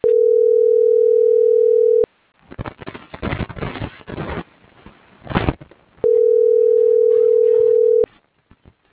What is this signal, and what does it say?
Ringback tone (North America), 440 + 480 Hz.